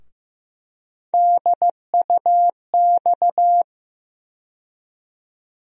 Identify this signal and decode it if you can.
Morse code, heard as a keyed tone, about 15 wpm, 700 Hz. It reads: DUX
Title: USA Call Sign Postfixes - 3 Letters 15wpm